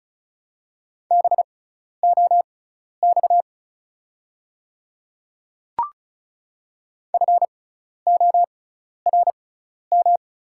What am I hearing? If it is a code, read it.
Morse code, heard as a keyed tone, about 36 wpm, spaced as for 12 wpm, 700 Hz. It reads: BOX FORM
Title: 6x Character Spacing - Top 500 Words - 35wpm